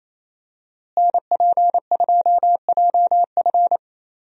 Morse code, heard as a keyed tone, about 28 wpm, 700 Hz. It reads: NP2JF
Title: USA Call Signs - 5 Characters - Repeated 2x - 28wpm